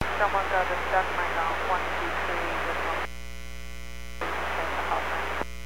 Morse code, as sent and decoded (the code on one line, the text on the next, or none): none